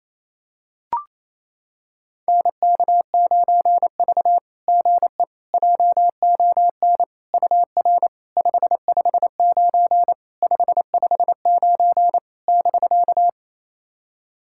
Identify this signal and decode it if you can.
Morse code, heard as a keyed tone, about 29 wpm, 690 Hz. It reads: NK9V GE JON UR 559 559 <BK>
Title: SOTA QSO Practice 28wpm